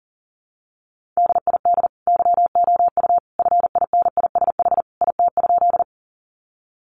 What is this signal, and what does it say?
Morse code, heard as a keyed tone, about 40 wpm, 700 Hz. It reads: DID YOU FINISH IT?
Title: Sentences from Top 1200 Words 40wpm